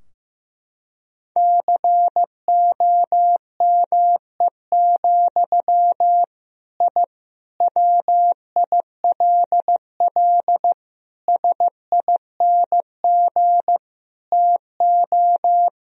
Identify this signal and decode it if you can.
Morse code, heard as a keyed tone, about 15 wpm, 700 Hz. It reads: COME, I WILL SING TO
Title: Sentences from Top 1000 Words - Review Part 2 - 15wpm